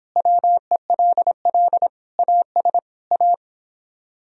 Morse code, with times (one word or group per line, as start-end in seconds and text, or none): 0.2-1.9 WELL
2.2-2.8 AS
3.1-3.3 A